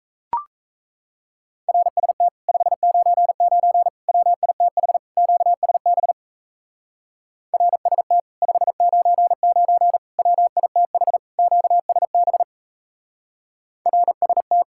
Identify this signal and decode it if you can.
Morse code, heard as a keyed tone, about 33 wpm, 700 Hz. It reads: RST 599 WITH QSB RST 599 WITH QSB RST
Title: Speed-Racing Common QSO Phrases - 28wpm